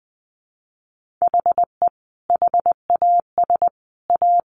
Morse code, heard as a keyed tone, about 20 wpm, 700 Hz. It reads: HE HAS A
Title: Sentences from Top 1400 Words 20wpm